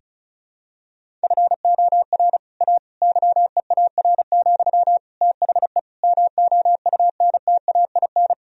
Morse code, heard as a keyed tone, about 34 wpm, 700 Hz. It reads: FOR A YEAR, THE MOUNTAIN
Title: Sentences from Top 1000 Words - Review Part 1 - 35wpm